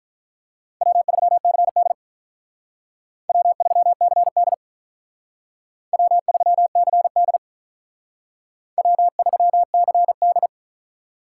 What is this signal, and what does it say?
Morse code, heard as a keyed tone, about 46 wpm, 700 Hz. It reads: W3CD W3CD W3CD W3CD